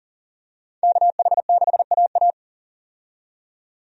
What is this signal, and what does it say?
Morse code, heard as a keyed tone, about 37 wpm, 700 Hz. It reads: KH6AA
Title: DXCC-100 Call Signs - 5 Characters - 40wpm